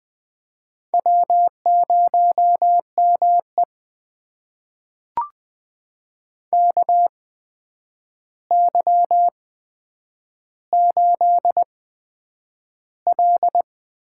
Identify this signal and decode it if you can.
Morse code, heard as a keyed tone, about 20 wpm, spaced as for 9 wpm, 700 Hz. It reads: W0ME KY8L
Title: USA Call Signs - 4 Characters - 8x Character Spacing - 20wpm